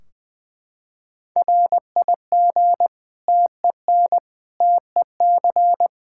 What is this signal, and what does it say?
Morse code, heard as a keyed tone, about 20 wpm, 700 Hz. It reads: RIG TEN TEC
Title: Practice QSO - Exchange 2 - Info - 20wpm